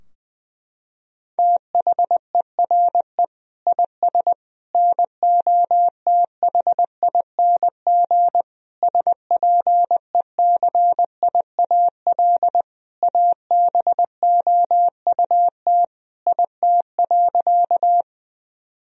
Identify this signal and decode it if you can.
Morse code, heard as a keyed tone, about 20 wpm, 700 Hz. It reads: THERE IS NOTHING SPECIAL ABOUT IT.
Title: Sentences from Top 1000 Words - Review Part 3 - 20wpm